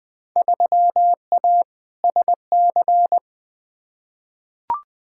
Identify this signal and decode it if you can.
Morse code, heard as a keyed tone, about 20 wpm, 700 Hz. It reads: 3A SC
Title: Field Day Exchanges - 20wpm